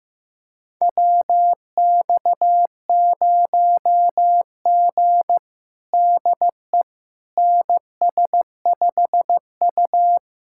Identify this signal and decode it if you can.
Morse code, heard as a keyed tone, about 16 wpm, 700 Hz. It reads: WX0G DE NS5U
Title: USA Callsign DE Callsign - 4 Characters - 15wpm